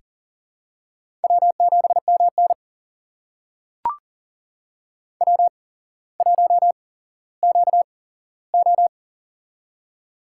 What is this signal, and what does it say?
Morse code, heard as a keyed tone, about 38 wpm, spaced as for 17 wpm, 700 Hz. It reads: W7MN W1QO
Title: USA Call Signs - 4 Characters - 8x Character Spacing - 40wpm